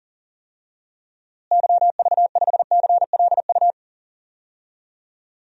Morse code, heard as a keyed tone, about 40 wpm, 700 Hz. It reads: YV5CLU
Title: DXCC-100 Call Signs - 6 Characters - Repeated 2x - 40wpm